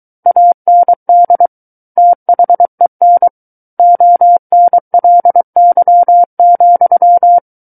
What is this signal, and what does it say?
Morse code, heard as a keyed tone, about 23 wpm, 700 Hz. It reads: AND THEN ONLY,
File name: Book One / 14-Chapter 14 - 23 WPM